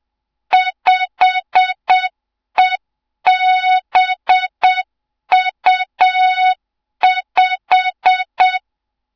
Morse code, heard as a keyed tone, about 7 wpm, 3700 Hz. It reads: 5EBU5
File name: Morze-HA2MN / hang / 07 GYAKORLAT